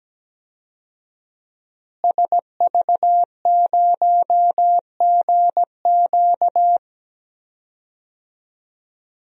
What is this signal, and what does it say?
Morse code, heard as a keyed tone, about 17 wpm, 700 Hz. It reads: SV0GQ
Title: DXCC-100 Call Signs - 5 Characters - Repeated 3x - 17wpm